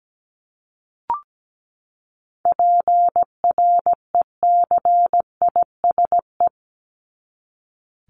Morse code, heard as a keyed tone, about 17 wpm, 700 Hz. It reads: PRECISE
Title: LY Suffix - Top 100 Words with Base Words 17wpm